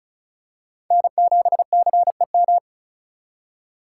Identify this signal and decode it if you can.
Morse code, heard as a keyed tone, about 34 wpm, 700 Hz. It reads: N7CEM
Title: USA Call Signs - 5 Characters - 2x Character Spacing - 35wpm